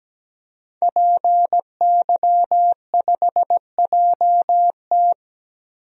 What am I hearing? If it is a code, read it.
Morse code, heard as a keyed tone, about 17 wpm, 700 Hz. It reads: PY5JT
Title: DXCC-100 Call Signs - 5 Characters - 17wpm